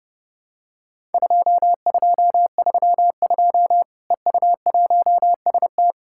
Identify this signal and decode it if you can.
Morse code, heard as a keyed tone, about 30 wpm, 700 Hz. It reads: 2232 EU1ST
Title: Worked All Europe DX Contest - QTC Exchange - 30wpm